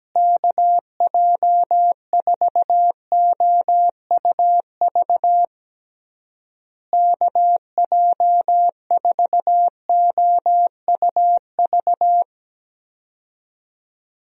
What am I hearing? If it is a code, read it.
Morse code, heard as a keyed tone, about 17 wpm, 700 Hz. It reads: KJ4OUV KJ4OUV
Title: USA Call Signs - 6 Characters - Repeated 3x - 17wpm